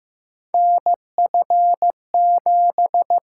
Morse code, heard as a keyed tone, about 15 wpm, 700 Hz. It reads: NF7
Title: USA Call Signs - 6x Character Spacing - 15wpm